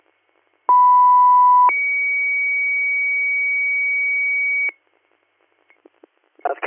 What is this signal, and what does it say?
Two-tone page, heard as a one-second tone, then a three-second tone, about 990 then 2250 Hz. It is heard from 0.7 to 4.7 s.